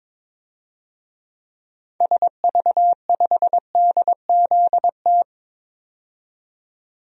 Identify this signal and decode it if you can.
Morse code, heard as a keyed tone, about 22 wpm, 700 Hz. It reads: SV5DZT
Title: DXCC-100 Call Signs - 6 Characters - 22wpm